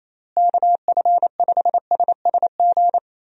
Morse code, heard as a keyed tone, about 28 wpm, 700 Hz. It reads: KF5SSG
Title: USA Call Signs - 6 Characters - Repeated 3x - 28wpm